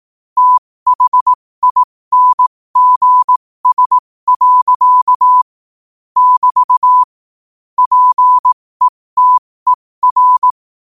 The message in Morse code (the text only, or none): THINGS. <BT> PETER